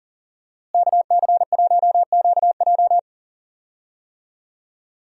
Morse code, heard as a keyed tone, about 40 wpm, 700 Hz. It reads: KC1QJ
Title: USA Call Signs - 5 Characters - Repeated 2x - 40wpm